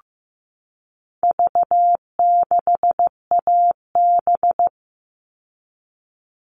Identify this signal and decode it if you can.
Morse code, heard as a keyed tone, about 15 wpm, 700 Hz. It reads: V6AB
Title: All Cut Numbers - 4-Digit 15wpm